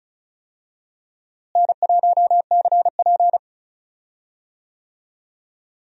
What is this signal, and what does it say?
Morse code, heard as a keyed tone, about 36 wpm, 700 Hz. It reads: N1CP
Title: USA Call Signs - 4 Characters - Repeated 2x - 35wpm